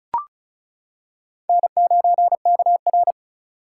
Morse code, heard as a keyed tone, about 34 wpm, 700 Hz. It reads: N9KR
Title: K1USN SST QSO Practice - 35wpm